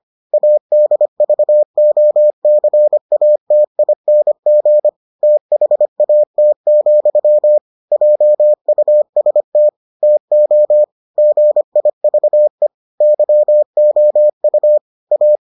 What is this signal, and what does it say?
Morse code, heard as a keyed tone, about 25 wpm, 600 Hz. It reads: ADVOCATING THAT, JUST TO GIVE YOU A